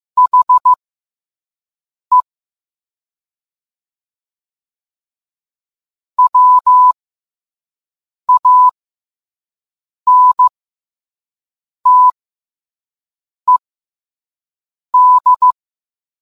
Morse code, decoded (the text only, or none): HE WANTED